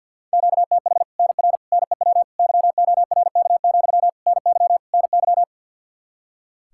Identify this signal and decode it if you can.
Morse code, heard as a keyed tone, about 50 wpm, 700 Hz. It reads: QTH NR NEW YORK, NY NY
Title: Practice QSO - Exchange 1 - Beginning - 50wpm